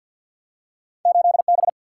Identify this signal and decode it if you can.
Morse code, heard as a keyed tone, about 53 wpm, 700 Hz. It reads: 8 B
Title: Single-Letter Single-Number 50wpm